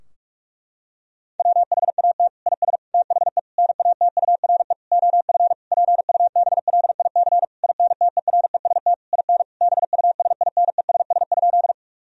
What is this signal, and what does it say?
Morse code, heard as a keyed tone, about 47 wpm, 700 Hz. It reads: WHAT IS THE NATURE OF PUBLIC INTEREST IN BUSINESS?
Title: Sentences from Top 1000 Words - Review Part 3 - 45wpm